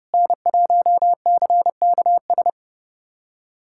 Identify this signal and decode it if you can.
Morse code, heard as a keyed tone, about 30 wpm, 700 Hz. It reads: N1CKS
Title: USA Callsign DE Callsign - 5 Characters - 30wpm